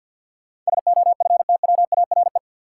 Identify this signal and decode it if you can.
Morse code, heard as a keyed tone, about 53 wpm, 700 Hz. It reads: SOFTWARE
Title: Sentences from Top 1000 Words - Review Part 3 - New Word Drill - 50wpm